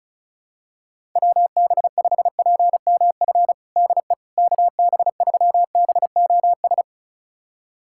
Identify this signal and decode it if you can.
Morse code, heard as a keyed tone, about 36 wpm, 700 Hz. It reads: WB5PMF DE KB3BOS
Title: USA Callsign DE Callsign 35wpm